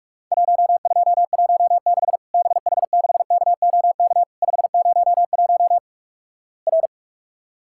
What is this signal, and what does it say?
Morse code, heard as a keyed tone, about 44 wpm, 700 Hz. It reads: 1216 BH6KOK 501 R
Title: Worked All Europe DX Contest - QTC Exchange - 45wpm